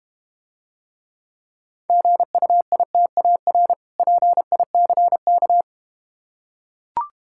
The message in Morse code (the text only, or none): GUITAR PICK